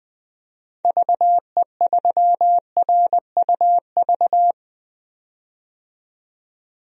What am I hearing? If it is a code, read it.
Morse code, heard as a keyed tone, about 20 wpm, 700 Hz. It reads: VE3RUV
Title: DXCC-100 Call Signs - 6 Characters - 20wpm